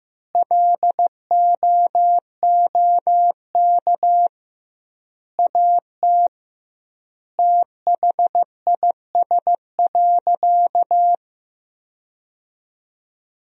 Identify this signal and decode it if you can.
Morse code, heard as a keyed tone, about 16 wpm, 700 Hz. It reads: LOOK AT THIS.
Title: 2x Word-Spacing - Sentences from Top 300 Words - 15wpm